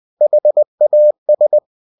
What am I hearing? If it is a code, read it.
Morse code, heard as a keyed tone, about 20 wpm, 600 Hz. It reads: HAS
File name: cw-books / winnie-the-pooh / winnie-20wpm-page-0040